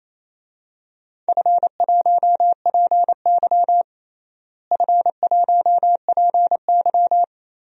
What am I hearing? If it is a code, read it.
Morse code, heard as a keyed tone, about 28 wpm, 700 Hz. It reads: F1PY F1PY
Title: DXCC-100 Call Signs - 4 Characters - Repeated 3x - 28wpm